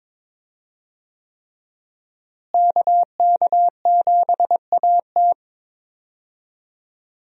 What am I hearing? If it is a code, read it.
Morse code, heard as a keyed tone, about 22 wpm, 700 Hz. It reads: KK7AT
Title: USA Call Signs - 5 Characters - Repeated 3x - 22wpm